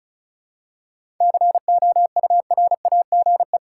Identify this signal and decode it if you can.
Morse code, heard as a keyed tone, about 34 wpm, 700 Hz. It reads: COURAGE
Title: Seven Letter Words - 35wpm